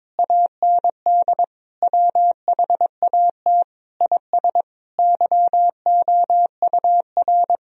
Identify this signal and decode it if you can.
Morse code, heard as a keyed tone, about 22 wpm, 700 Hz. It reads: AND WHAT IS YOUR
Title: Sentences from Top 1000 Words - Review Part 4 - 22wpm